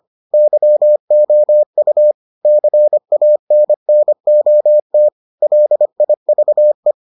Morse code, heard as a keyed tone, about 25 wpm, 600 Hz. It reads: YOU CANNOT LIVE